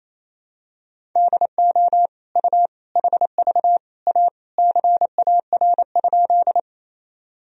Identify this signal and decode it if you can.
Morse code, heard as a keyed tone, about 28 wpm, 700 Hz. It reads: DO U HV A CAR?
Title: Sentences from Top 300 Words and Abbreviations 28wpm